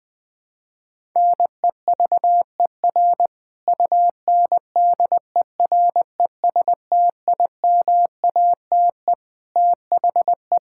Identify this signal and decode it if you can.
Morse code, heard as a keyed tone, about 21 wpm, 700 Hz. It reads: NEVER UNDERESTIMATE THE
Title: Sentences with 4-Word Phrases - Spoken Beginning - 20wpm